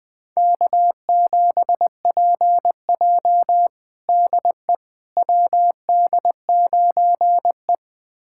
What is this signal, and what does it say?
Morse code, heard as a keyed tone, about 20 wpm, 700 Hz. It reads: K7PJ DE WD9E
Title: USA Callsign DE Callsign - 4 Characters - 20wpm